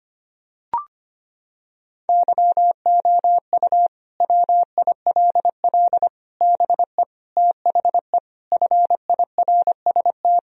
Morse code, heard as a keyed tone, about 25 wpm, 700 Hz. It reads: YOU WILL BE THE FIRST
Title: Sentences from Top 1000 Words - Review Part 4 - 25wpm